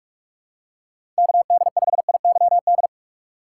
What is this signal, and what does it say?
Morse code, heard as a keyed tone, about 46 wpm, 700 Hz. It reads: KD5IYD
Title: USA Call Signs - 6 Characters - 2x Character Spacing - 45wpm